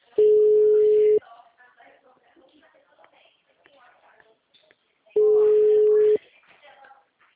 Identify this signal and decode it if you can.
Ringback tone (Europe, 425 Hz), with speech.
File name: out- -2018- - - .77477